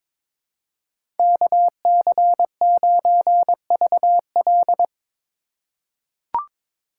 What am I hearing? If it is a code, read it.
Morse code, heard as a keyed tone, about 22 wpm, 700 Hz. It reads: KC9VL
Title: USA Call Signs - 5 Characters - 3x Character Spacing - 22wpm